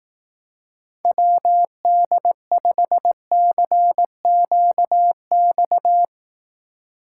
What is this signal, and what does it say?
Morse code, heard as a keyed tone, about 18 wpm, 700 Hz. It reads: WD5CQX
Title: Speed Racing - USA 2x3 Call Signs 15wpm